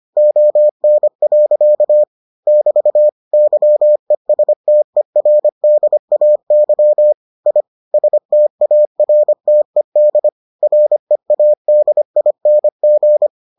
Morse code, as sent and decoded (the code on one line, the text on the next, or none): --- -. .-.-.- / -...- / -.-- . ... - . .-. -.. .- -.-- / .. / ... - .- .-. - . -.. / .-. .- -.. .. -. --.
ON. <BT> YESTERDAY I STARTED READING